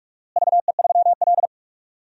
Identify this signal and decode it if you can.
Morse code, heard as a keyed tone, about 46 wpm, 700 Hz. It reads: VE3L
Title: DXCC-100 Call Signs - 4 Characters - Repeated 3x - 45wpm